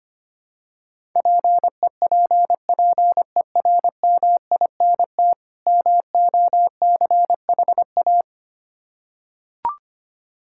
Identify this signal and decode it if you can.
Morse code, heard as a keyed tone, about 25 wpm, 700 Hz. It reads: PEPPERMINT MOCHA
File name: practice / 25wpm / Difficult Words with Hints - 25wpm